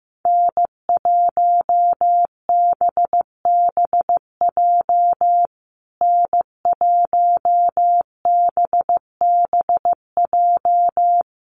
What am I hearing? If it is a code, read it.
Morse code, heard as a keyed tone, about 15 wpm, 700 Hz. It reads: N1BBJ N1BBJ
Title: USA Call Signs Repeated 3x 15wpm